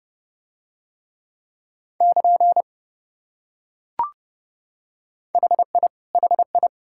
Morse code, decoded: <KN> HI HI

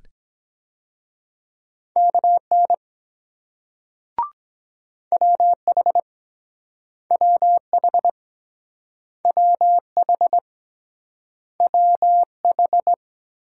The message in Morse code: KN WH WH WH WH

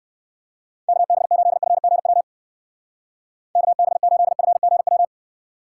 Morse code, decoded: KB8FGW KB8FGW

